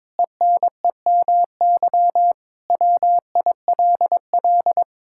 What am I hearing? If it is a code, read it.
Morse code, heard as a keyed tone, about 22 wpm, 700 Hz. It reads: ENEMY WILL
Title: Pangram Challenge v2 - 22wpm